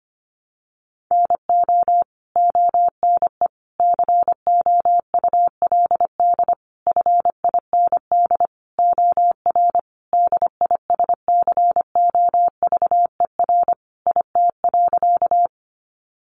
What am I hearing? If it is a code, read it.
Morse code, heard as a keyed tone, about 25 wpm, 700 Hz. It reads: NO ONE COULD FIND OR DISCOVER IT.
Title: Sentences from Top 1100 Words 25wpm